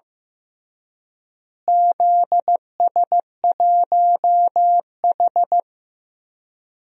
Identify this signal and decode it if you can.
Morse code, heard as a keyed tone, about 15 wpm, 700 Hz. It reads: ZS1H
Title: DXCC-100 Call Signs - 4 Characters - Repeated 3x - 15wpm